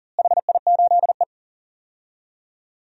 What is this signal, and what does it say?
Morse code, heard as a keyed tone, about 40 wpm, 700 Hz. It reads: HI8E
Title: DXCC-100 Call Signs - 4 Characters - Repeated 2x - 40wpm